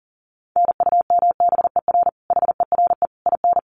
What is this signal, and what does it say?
Morse code, heard as a keyed tone, about 40 wpm, 700 Hz. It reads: NUMBER HERE IN
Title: Sets of 5 Words Encore2 40wpm